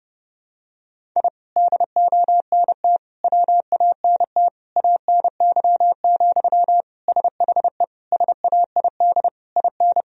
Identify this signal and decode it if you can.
Morse code, heard as a keyed tone, about 32 wpm, 700 Hz. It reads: I DONT WANT ANY, SHE SAID IN
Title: Sentences from Top 1000 Words - Review Part 1 - 30wpm